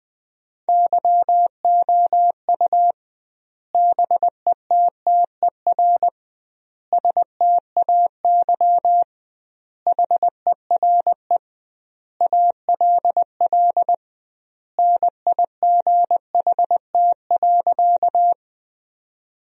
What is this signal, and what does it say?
Morse code, heard as a keyed tone, about 20 wpm, 700 Hz. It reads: YOU BETTER STAY HERE ALL NIGHT.
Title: 2x Word-Spacing - Sentences from Top 500 Words - 20wpm